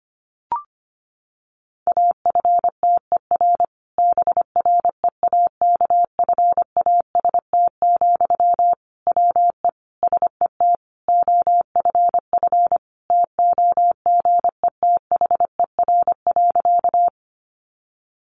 Morse code, heard as a keyed tone, about 25 wpm, 700 Hz. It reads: AFTER BREAKFAST, WE SET OFF TOGETHER.
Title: Sentences from Top 1500 Words 25wpm